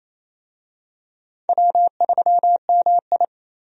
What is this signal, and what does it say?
Morse code, heard as a keyed tone, about 29 wpm, 690 Hz. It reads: W3MI
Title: USA Call Signs - 4 Characters - 28wpm